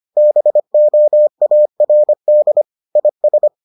Morse code, heard as a keyed tone, about 25 wpm, 600 Hz. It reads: BOARD IS